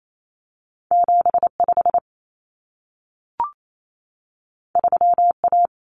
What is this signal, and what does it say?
Morse code, heard as a keyed tone, about 28 wpm, 700 Hz. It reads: 75 3A
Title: Common Cut Numbers - 2-Digit 28wpm